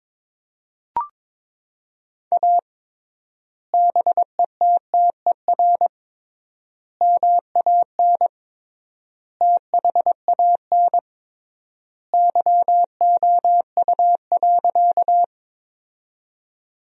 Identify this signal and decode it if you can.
Morse code, heard as a keyed tone, about 22 wpm, 700 Hz. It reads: A BETTER MAN THAN YOU.
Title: 3x Word-Spacing - Sentences from Top 400 Words - 22wpm